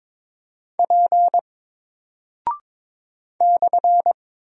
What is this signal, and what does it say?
Morse code, heard as a keyed tone, about 22 wpm, 700 Hz. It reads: P /
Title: Single Character (Letter:Number:Punctuation:Prosign) Alt 22wpm